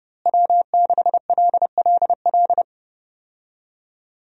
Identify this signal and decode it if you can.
Morse code, heard as a keyed tone, about 30 wpm, 700 Hz. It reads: W6LLL
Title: USA Call Signs - 5 Characters - Repeated 3x - 30wpm